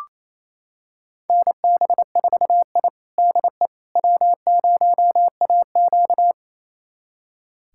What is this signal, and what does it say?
Morse code, heard as a keyed tone, about 28 wpm, 700 Hz. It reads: NB4I DE W0AQ
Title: USA Callsign DE Callsign - 4 Characters - 28wpm